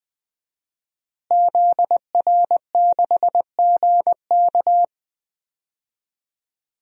Morse code, heard as a keyed tone, about 20 wpm, 700 Hz. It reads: ZR6GK
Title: DXCC-100 Call Signs - 5 Characters - Repeated 3x - 20wpm